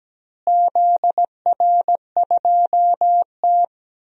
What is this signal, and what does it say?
Morse code, heard as a keyed tone, about 17 wpm, 700 Hz. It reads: ZR2T
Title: DXCC-100 Call Signs - 4 Characters - Repeated 3x - 17wpm